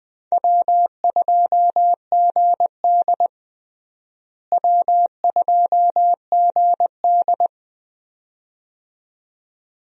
Morse code, heard as a keyed tone, about 20 wpm, 700 Hz. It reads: W2GD W2GD
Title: USA Call Signs - 4 Characters - Repeated 2x - 20wpm